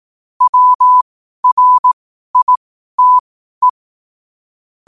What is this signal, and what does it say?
Morse code, heard as a keyed tone, about 18 wpm, spaced as for 12 wpm, 1000 Hz. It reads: WRITE
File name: qotdfiles / 12wpm